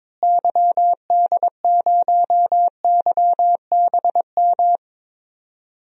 Morse code, heard as a keyed tone, about 21 wpm, 700 Hz. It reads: YD0YBM
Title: DXCC-100 Call Signs - 6 Characters - Repeated 3x - 22wpm